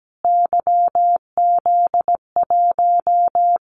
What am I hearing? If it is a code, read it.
Morse code, heard as a keyed tone, about 17 wpm, 700 Hz. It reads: YZ1
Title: Callsign Prefixes for Top 25 DXCC - Repeated 3x - 17wpm